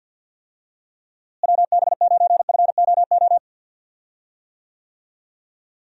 Morse code, heard as a keyed tone, about 53 wpm, 700 Hz. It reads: WB9FOO
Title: USA Call Signs - 6 Characters - Repeated 3x - 50wpm